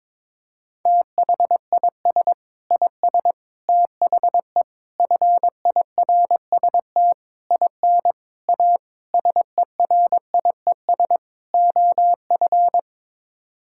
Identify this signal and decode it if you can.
Morse code, heard as a keyed tone, about 22 wpm, 700 Hz. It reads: THIS IS THE FIRST IN A SERIES OF
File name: practice / 22wpm / Sets of 8 Words - 22wpm